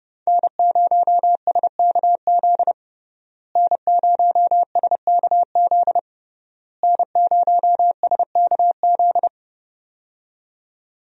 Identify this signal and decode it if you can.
Morse code, heard as a keyed tone, about 30 wpm, 700 Hz. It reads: N0SKZ N0SKZ N0SKZ
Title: USA Call Signs - 5 Characters - Repeated 3x - 30wpm